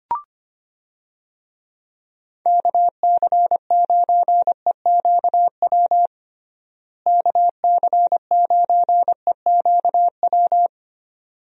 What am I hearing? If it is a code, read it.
Morse code, heard as a keyed tone, about 25 wpm, 700 Hz. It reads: KC9EQW KC9EQW
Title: USA Call Signs - 6 Characters - Repeated 2x - 25wpm